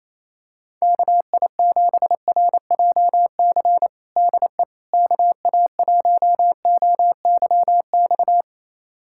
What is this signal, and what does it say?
Morse code, heard as a keyed tone, about 27 wpm, 700 Hz. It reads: KI7RJC DE KA1OYX
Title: USA Callsign DE Callsign - 6 Characters - 28wpm